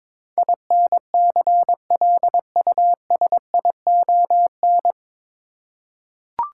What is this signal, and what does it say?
Morse code, heard as a keyed tone, about 22 wpm, 700 Hz. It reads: INCLUSION
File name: 9 Letter Words - First 3 Letters Given - 22wpm